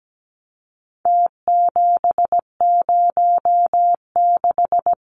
Morse code, heard as a keyed tone, about 17 wpm, 700 Hz. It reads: T706
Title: Numbers - Contest Style - 4-Digit 17wpm